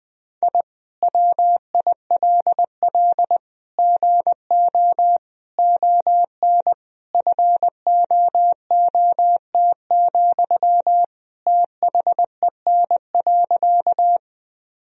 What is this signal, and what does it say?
Morse code, heard as a keyed tone, about 20 wpm, 700 Hz. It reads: I WILL GO ON FOOT, THEN.
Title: Sentences from Top 1000 Words - Review Part 3 - 20wpm